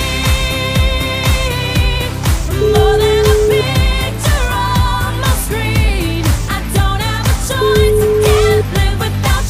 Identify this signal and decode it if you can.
Ringback tone (Europe, 425 Hz), with noise.